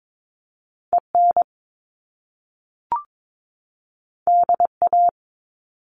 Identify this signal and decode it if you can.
Morse code, heard as a keyed tone, about 22 wpm, 700 Hz. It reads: EN DA
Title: Top Two Letter Combinations in Words 22wpm